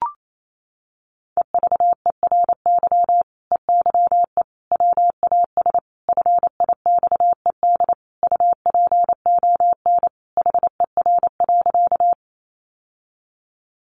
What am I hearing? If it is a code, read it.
Morse code, heard as a keyed tone, about 28 wpm, 700 Hz. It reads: EVERY EYE WAS FIXED UPON HER.
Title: Sentences from Top 1100 Words 28wpm